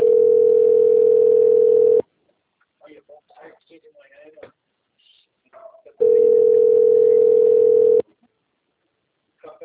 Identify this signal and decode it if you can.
Ringback tone (North America), 440 + 480 Hz, with speech.
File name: rg-1600-unknown- - - .1057